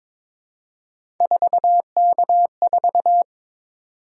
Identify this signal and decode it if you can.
Morse code, heard as a keyed tone, about 22 wpm, 700 Hz. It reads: 4K4
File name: practice / 22wpm / Three Letter-Number Combinations - 22wpm